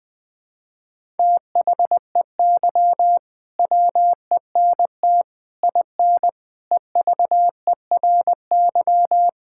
Morse code, heard as a keyed tone, about 20 wpm, 700 Hz. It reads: THEY WENT IN EVERY
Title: Sentences from Top 1000 Words - Review Part 5 - 20wpm